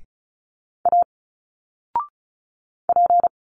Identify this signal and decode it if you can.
Morse code, heard as a keyed tone, about 33 wpm, 700 Hz. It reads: A P
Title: Single Letters 35wpm